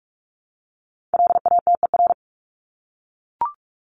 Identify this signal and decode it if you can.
Morse code, heard as a keyed tone, about 44 wpm, 700 Hz. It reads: LATER
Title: Top 501-750 Words 45wpm